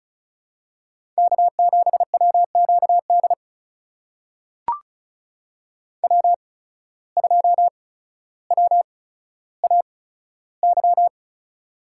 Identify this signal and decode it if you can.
Morse code, heard as a keyed tone, about 36 wpm, spaced as for 15 wpm, 700 Hz. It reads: K7WQD W2WAY
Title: USA Call Signs - 8x Character Spacing - 35wpm